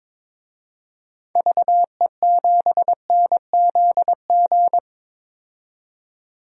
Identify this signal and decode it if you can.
Morse code, heard as a keyed tone, about 22 wpm, 700 Hz. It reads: VE7NZG
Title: DXCC-100 Call Signs - 6 Characters - 22wpm